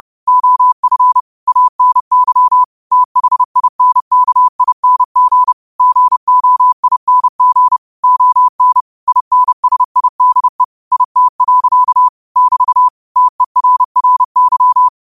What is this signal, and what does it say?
Morse code, heard as a keyed tone, about 30 wpm, 1000 Hz. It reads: OF ANY THINKING GOING ON INSIDE IT. <BT> TERRY
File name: qotdfiles / 30wpm